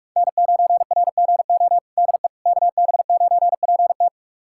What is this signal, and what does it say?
Morse code, heard as a keyed tone, about 46 wpm, 700 Hz. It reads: N9RGO DE KB9PT
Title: USA Callsign DE Callsign - 5 Characters - 45wpm